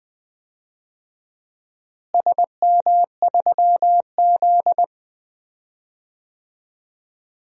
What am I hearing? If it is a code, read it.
Morse code, heard as a keyed tone, about 21 wpm, 700 Hz. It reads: SM3Z
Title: DXCC-100 Call Signs - 4 Characters - Repeated 2x - 20wpm